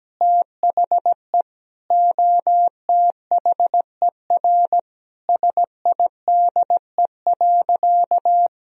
Morse code, heard as a keyed tone, about 17 wpm, 700 Hz. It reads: THE OTHER SIDE.
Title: Sentences from Top 1000 Words - Review Part 1 - 17wpm